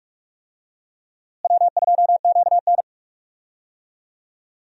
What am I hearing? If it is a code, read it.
Morse code, heard as a keyed tone, about 46 wpm, 700 Hz. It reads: W2QN
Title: USA Call Signs - 4 Characters - Repeated 3x - 45wpm